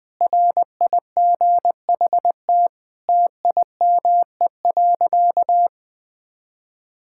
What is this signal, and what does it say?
Morse code, heard as a keyed tone, about 19 wpm, 700 Hz. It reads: RIGHT TIME.
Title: Sentences with 6-Word Phrases - 20wpm